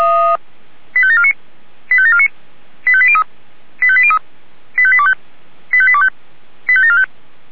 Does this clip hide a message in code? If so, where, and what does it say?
Yes: five-tone selective call (ZVEI) from 0.9 to 7.0 s, heard as quick runs of five tones, tones of about 70 ms. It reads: 85749 85749 85803 85803 85825 85825 85745